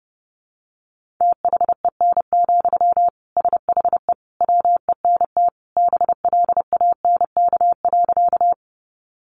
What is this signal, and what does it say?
Morse code, heard as a keyed tone, about 30 wpm, 700 Hz. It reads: THEN, SHE WENT BLANK.
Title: Sentences from Top 1500 Words 30wpm